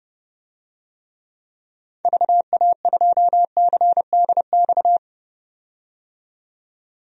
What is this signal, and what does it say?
Morse code, heard as a keyed tone, about 30 wpm, 700 Hz. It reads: VA2CDX
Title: DXCC-100 Call Signs - 6 Characters - 30wpm